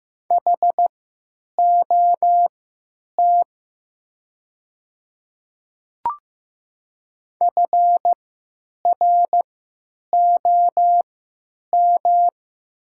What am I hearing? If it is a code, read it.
Morse code, heard as a keyed tone, about 15 wpm, spaced as for 9 wpm, 700 Hz. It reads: HOT FROM
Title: 3x Character Spacing - Top 100 Words - 15wpm